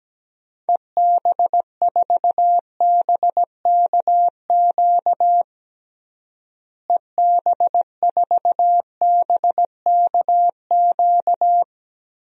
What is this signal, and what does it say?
Morse code, heard as a keyed tone, about 17 wpm, 700 Hz. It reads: EB4BKQ EB4BKQ
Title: DXCC-100 Call Signs - 6 Characters - Repeated 2x - 17wpm